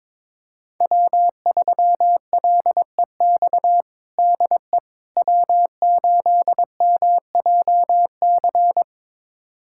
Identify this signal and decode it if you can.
Morse code, heard as a keyed tone, about 21 wpm, 700 Hz. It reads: W3LEX DE W8MJC
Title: USA Callsign DE Callsign - 5 Characters - 22wpm